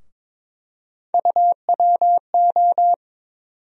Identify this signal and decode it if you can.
Morse code, heard as a keyed tone, about 22 wpm, 700 Hz. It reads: UWO
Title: USA Call Sign Postfixes - 3 Letters 22wpm